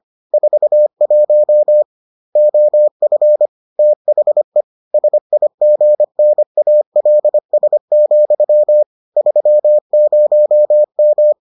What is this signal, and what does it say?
Morse code, heard as a keyed tone, about 25 wpm, 600 Hz. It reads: 41 OF THE SIGNALS, 30M